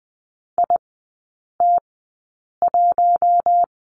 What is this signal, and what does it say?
Morse code, heard as a keyed tone, about 20 wpm, spaced as for 8 wpm, 700 Hz. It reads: IT1
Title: 041 Lesson 8e - Number 1 - Three Characters 20wpm